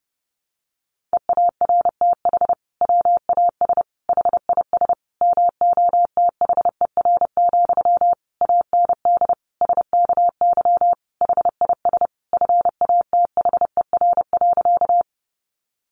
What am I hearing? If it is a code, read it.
Morse code, heard as a keyed tone, about 30 wpm, 700 Hz. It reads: EARTH WAS HIS MOTHER, AND SKY HIS FATHER.
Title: Sentences from Top 1200 Words 30wpm